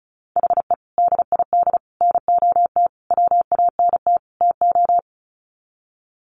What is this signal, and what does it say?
Morse code, heard as a keyed tone, about 34 wpm, 700 Hz. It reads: HE DID NOT WANT TO
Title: Sets of 5 Words 35wpm